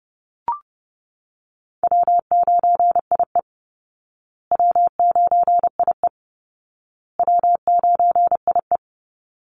Morse code, heard as a keyed tone, about 30 wpm, 700 Hz. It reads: W9IE W9IE W9IE
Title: Easy Call Signs Repeated 3x - 30wpm